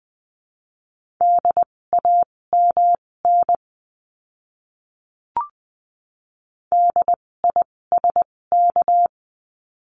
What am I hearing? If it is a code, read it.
Morse code, heard as a keyed tone, about 20 wpm, spaced as for 16 wpm, 700 Hz. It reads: DAMN DISK
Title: 256 Lesson 42a - Words 12-20wpm